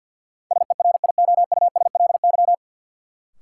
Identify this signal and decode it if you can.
Morse code, heard as a keyed tone, about 53 wpm, 700 Hz. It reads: SERIOUSLY
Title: Sentences from Top 1200 Words - New Word Drill Alt - 50wpm